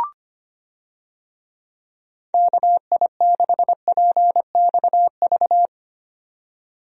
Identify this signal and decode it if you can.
Morse code, heard as a keyed tone, about 25 wpm, 700 Hz. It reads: KI6PXV
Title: USA Call Signs - 6 Characters - 25wpm